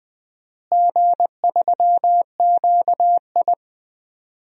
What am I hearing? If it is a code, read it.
Morse code, heard as a keyed tone, about 21 wpm, 700 Hz. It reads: G3QI
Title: DXCC-100 Call Signs - 4 Characters - Repeated 3x - 20wpm